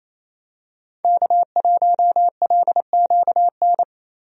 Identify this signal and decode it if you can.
Morse code, heard as a keyed tone, about 28 wpm, 700 Hz. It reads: K1LQN
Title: USA Call Signs - 5 Characters - Repeated 2x - 28wpm